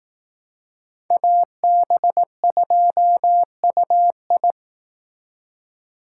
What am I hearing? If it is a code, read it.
Morse code, heard as a keyed tone, about 18 wpm, 700 Hz. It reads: AB2UI
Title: Speed Racing - USA Call Signs 15wpm